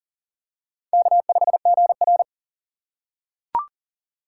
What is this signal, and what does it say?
Morse code, heard as a keyed tone, about 40 wpm, 700 Hz. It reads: K5GR
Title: USA Call Signs - 4x Character Spacing - 40wpm